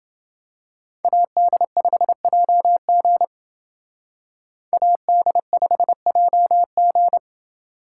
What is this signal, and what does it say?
Morse code, heard as a keyed tone, about 28 wpm, 700 Hz. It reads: AD5JG AD5JG